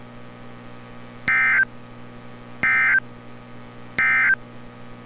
Emergency Alert System (SAME) tones, heard as harsh buzzing bursts. It is heard from 1.2 to 4.4 s.
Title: NOAA Weather Radio - Altoona, PA